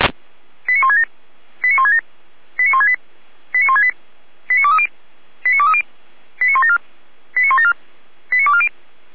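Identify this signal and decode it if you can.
Five-tone selective call (ZVEI), heard as quick runs of five tones, tones of about 70 ms. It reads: 89167 89167 89168 89168 89230 89230 89174 89174 89240
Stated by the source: ZVEI five-tone selective call received with an SDR.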